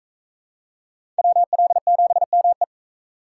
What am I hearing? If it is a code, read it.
Morse code, heard as a keyed tone, about 37 wpm, 700 Hz. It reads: WL7ME